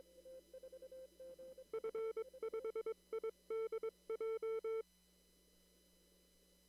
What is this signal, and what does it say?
Morse code, heard as a keyed tone, about 22 wpm, 450 Hz. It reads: F5IDJ
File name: CW / kt5j wpx cw-op-k5pi